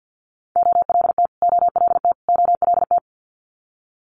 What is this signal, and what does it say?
Morse code, heard as a keyed tone, about 50 wpm, 700 Hz. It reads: OLT OLT OLT
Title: Multi-letter Phonemes Repeated 3x 50wpm